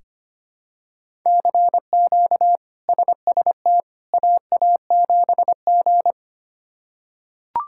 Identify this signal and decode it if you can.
Morse code, heard as a keyed tone, about 25 wpm, 700 Hz. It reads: CQ SST AA7G
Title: K1USN SST QSO Practice - 25wpm